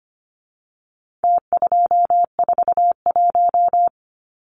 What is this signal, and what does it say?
Morse code, heard as a keyed tone, about 25 wpm, 700 Hz. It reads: T241
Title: Numbers - Contest Style - 4-Digit 25wpm